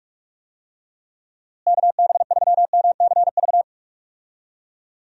Morse code, heard as a keyed tone, about 46 wpm, 700 Hz. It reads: KB3MCV
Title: USA Call Signs - 6 Characters - Repeated 2x - 45wpm